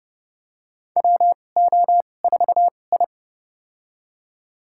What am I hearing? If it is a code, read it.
Morse code, heard as a keyed tone, about 32 wpm, spaced as for 22 wpm, 700 Hz. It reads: WO4I
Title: USA Call Signs - 2x Character Spacing - 30wpm